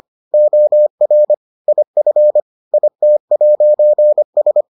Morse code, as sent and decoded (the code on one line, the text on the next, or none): --- .-. / .. ..-. / .. - .----. ...
OR IF IT'S